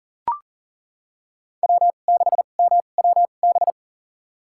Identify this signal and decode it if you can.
Morse code, heard as a keyed tone, about 40 wpm, spaced as for 29 wpm, 700 Hz. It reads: W6MWB
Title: USA Call Signs - 5 Characters - 2x Character Spacing - 40wpm